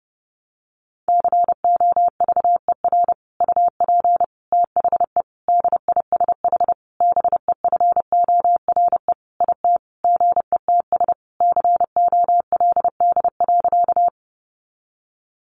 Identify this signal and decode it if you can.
Morse code, heard as a keyed tone, about 30 wpm, 700 Hz. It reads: COVER UP THE DISH BEFORE IT GETS COLD.
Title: Sentences from Top 1300 Words 30wpm